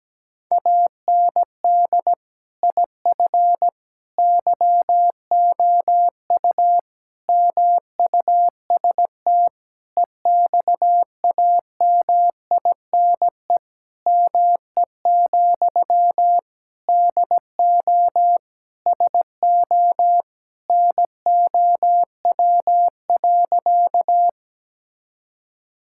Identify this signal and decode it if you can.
Morse code, heard as a keyed tone, about 17 wpm, 700 Hz. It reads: AND IF YOU MUST EXAMINE ME, DO SO NOW.
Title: Sentences from Top 1600 Words 17wpm